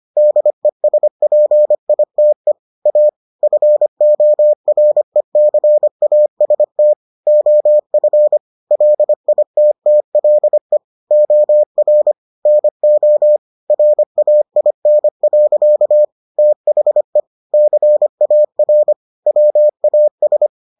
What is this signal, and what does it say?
Morse code, heard as a keyed tone, about 25 wpm, 600 Hz. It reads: DESPITE A FORECAST OF LITTLE OR NO RAIN. THE CAR WAS